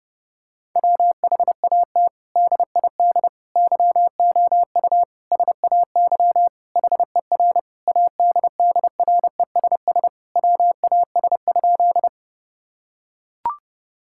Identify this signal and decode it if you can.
Morse code, heard as a keyed tone, about 30 wpm, 700 Hz. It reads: WHAT DID YOU SAY HER ADDRESS WAS?
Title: Sentences from Top 1000 Words - Review Part 6 - 30wpm